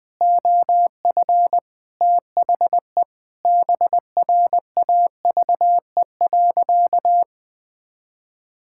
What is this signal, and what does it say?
Morse code, heard as a keyed tone, about 21 wpm, 700 Hz. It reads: OF THE BRAVE.
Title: Sentences with 8-Word Phrases - 20wpm